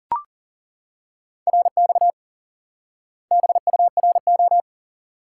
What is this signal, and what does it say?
Morse code, heard as a keyed tone, about 40 wpm, 700 Hz. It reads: RX BURO
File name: IQE Course 3 - Lesson 89 - YAGI - Two Elements - 40wpm